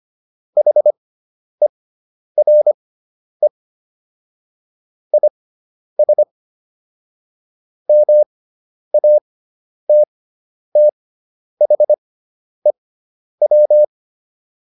Morse code, heard as a keyed tone, about 25 wpm, spaced as for 10 wpm, 600 Hz. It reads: HERE IS MATTHEW
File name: QSO303 10